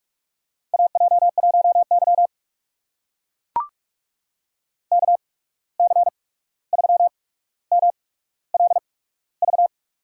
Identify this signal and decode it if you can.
Morse code, heard as a keyed tone, about 44 wpm, spaced as for 17 wpm, 700 Hz. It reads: AJ1Y KC3MLV
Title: USA Call Signs - 8x Character Spacing - 45wpm